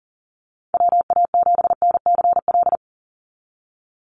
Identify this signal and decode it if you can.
Morse code, heard as a keyed tone, about 40 wpm, 700 Hz. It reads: WA7NCL
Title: NA SOTA Activator Callsigns 40wpm